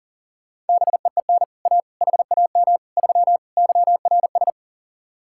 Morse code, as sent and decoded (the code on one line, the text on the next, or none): -... . -. / .- / .... .- -- / ...-- / -.-- .-. ...
BEEN A HAM 3 YRS